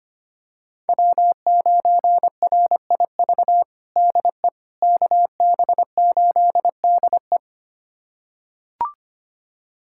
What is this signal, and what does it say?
Morse code, heard as a keyed tone, about 25 wpm, 700 Hz. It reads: W9RIV DE KB8DE
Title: USA Callsign DE Callsign - 5 Characters - 25wpm